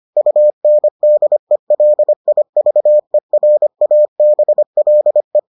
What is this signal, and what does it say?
Morse code, heard as a keyed tone, about 25 wpm, 600 Hz. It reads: UNDELIVERABLE